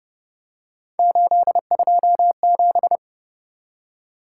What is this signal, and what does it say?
Morse code, heard as a keyed tone, about 29 wpm, 700 Hz. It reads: 827